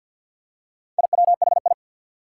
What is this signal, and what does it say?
Morse code, heard as a keyed tone, about 52 wpm, 700 Hz. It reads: IWHI
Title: Four Challenging Letter-Number Combinations 50wpm